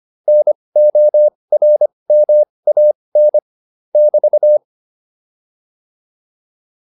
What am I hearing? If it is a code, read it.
Morse code, heard as a keyed tone, about 25 wpm, spaced as for 20 wpm, 600 Hz. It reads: NORMAN <BT>
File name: QSO209a 20